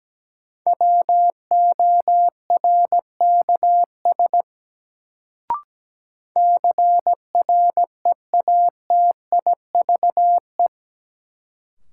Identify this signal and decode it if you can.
Morse code, heard as a keyed tone, about 17 wpm, 700 Hz. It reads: WORKS CREATIVE
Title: Sentences from Top 1000 Words - Review Part 5 - New Word Drill - 17wpm